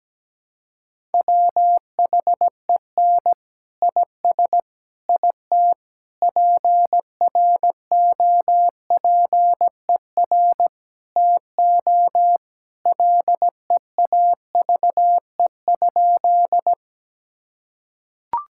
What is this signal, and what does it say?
Morse code, heard as a keyed tone, about 17 wpm, 700 Hz. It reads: WHEN IS IT PROPER TO LEAVE?
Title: Sentences from Top 1000 Words - Review Part 5 - 17wpm